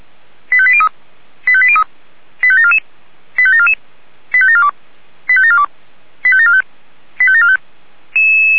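Five-tone selective call (ZVEI), heard as quick runs of five tones, tones of about 70 ms. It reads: 85803 85803 85744 85744 85742 85742 85745 85745
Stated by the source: ZVEI five-tone call received by SDR.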